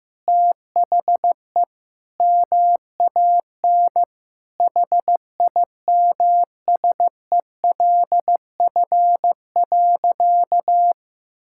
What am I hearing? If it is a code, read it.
Morse code, heard as a keyed tone, about 16 wpm, 700 Hz. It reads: THE MAN HIMSELF.
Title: Sentences from Top 1600 Words 15wpm